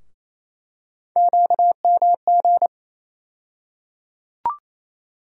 Morse code, heard as a keyed tone, about 28 wpm, 700 Hz. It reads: QMG
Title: USA Call Sign Postfixes - 3 Letters 28wpm